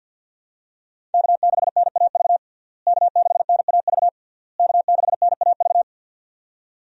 Morse code, heard as a keyed tone, about 52 wpm, 700 Hz. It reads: K6NAV K6NAV K6NAV
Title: USA Call Signs - 5 Characters - Repeated 3x - 50wpm